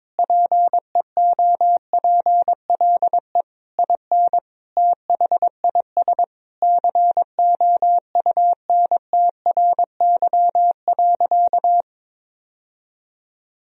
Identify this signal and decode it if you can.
Morse code, heard as a keyed tone, about 22 wpm, 700 Hz. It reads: PEOPLE IN THIS COUNTRY.
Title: Sentences from Top 1000 Words - Review Part 3 - 22wpm